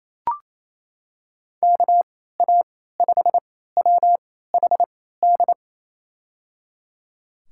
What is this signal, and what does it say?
Morse code, heard as a keyed tone, about 28 wpm, spaced as for 16 wpm, 700 Hz. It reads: KA5WHD
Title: USA Call Signs - 6 Characters - 3x Character Spacing - 28wpm